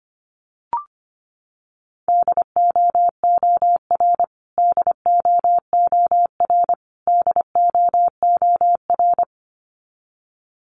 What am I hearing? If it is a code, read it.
Morse code, heard as a keyed tone, about 25 wpm, 700 Hz. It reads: DOOR DOOR DOOR
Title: Top 300 Words Repeated 25wpm